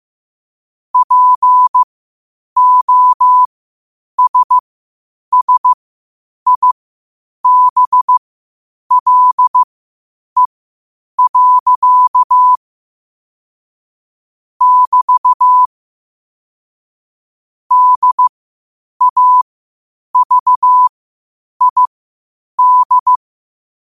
Morse code, heard as a keyed tone, about 15 wpm, spaced as for 8 wpm, 1000 Hz. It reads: POSSIBLE. <BT> DAVID
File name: qotdfiles / 8wpm